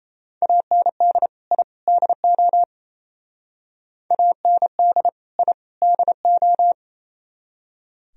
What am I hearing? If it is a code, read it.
Morse code, heard as a keyed tone, about 35 wpm, 700 Hz. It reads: AND I DO AND I DO